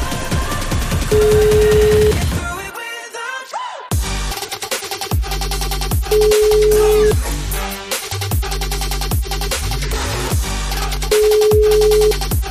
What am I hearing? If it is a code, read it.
Ringback tone (Europe, 425 Hz), with noise.